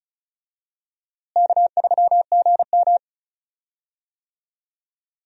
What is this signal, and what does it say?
Morse code, heard as a keyed tone, about 36 wpm, 700 Hz. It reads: K3GM
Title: USA Call Signs - 4 Characters - Repeated 3x - 35wpm